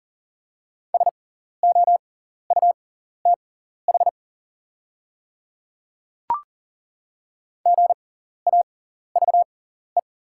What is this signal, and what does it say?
Morse code, heard as a keyed tone, about 40 wpm, spaced as for 14 wpm, 700 Hz. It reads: SOUTH GAVE
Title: 6x Character Spacing - Top 500 Words - 40wpm